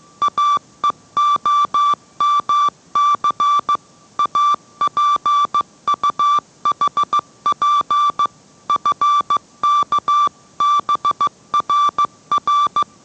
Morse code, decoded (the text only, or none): AEOMC APUHP FKBRR